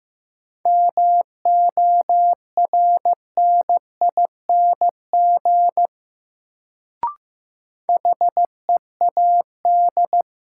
Morse code, heard as a keyed tone, about 15 wpm, 700 Hz. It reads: MORNING HEAD
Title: Sets of 3 Words - New Word Drill - 15wpm